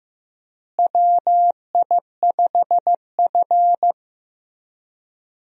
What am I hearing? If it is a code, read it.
Morse code, heard as a keyed tone, about 15 wpm, 700 Hz. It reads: WI5F
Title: USA Call Signs - 4 Characters - Repeated 3x - 15wpm